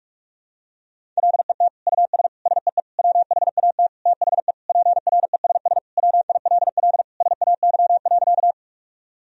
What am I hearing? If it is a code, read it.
Morse code, heard as a keyed tone, about 45 wpm, 700 Hz. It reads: LET US SEE WHAT THE PRESS WILL SAY.
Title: Sentences from Top 1000 Words - Review Part 1 - 45wpm